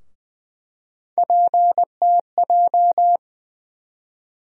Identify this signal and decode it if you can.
Morse code, heard as a keyed tone, about 21 wpm, 700 Hz. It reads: PTJ
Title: USA Call Sign Postfixes - 3 Letters 20wpm